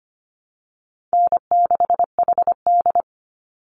Morse code, heard as a keyed tone, about 25 wpm, 700 Hz. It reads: N6HD
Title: Call Signs Repeated 3x - 25wpm